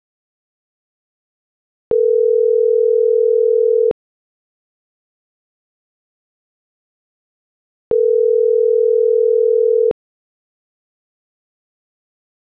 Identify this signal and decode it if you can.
Ringback tone (North America), 440 + 480 Hz.